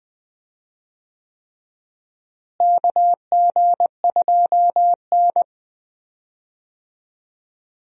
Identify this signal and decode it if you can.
Morse code, heard as a keyed tone, about 20 wpm, 700 Hz. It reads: KG2N
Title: USA Call Signs - 4 Characters - Repeated 3x - 20wpm